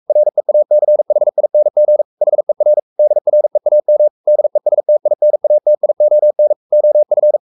Morse code, heard as a keyed tone, about 44 wpm, 600 Hz. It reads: REACHING HER DREAM DESTINATION OF
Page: Romanen Never af Ken Follet på engelsk.